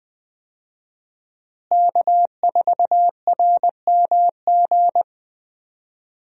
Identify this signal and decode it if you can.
Morse code, heard as a keyed tone, about 20 wpm, 700 Hz. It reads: K4RMG
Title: USA Call Signs - 5 Characters - Repeated 3x - 20wpm